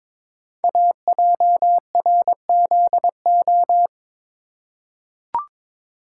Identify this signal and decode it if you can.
Morse code, heard as a keyed tone, about 22 wpm, 700 Hz. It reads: AJRZO
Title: Code Groups - 5 Letter:Numbers - 22wpm